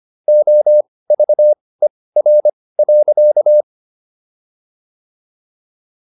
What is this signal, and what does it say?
Morse code, heard as a keyed tone, about 25 wpm, spaced as for 18 wpm, 600 Hz. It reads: OVER.